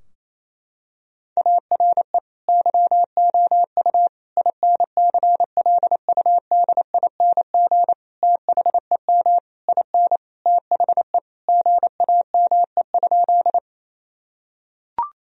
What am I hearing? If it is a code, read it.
Morse code, heard as a keyed tone, about 28 wpm, 700 Hz. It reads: ARE YOU INCLUDING THEM IN THE GAME?
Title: Sentences from Top 1000 Words - Review Part 4 - 28wpm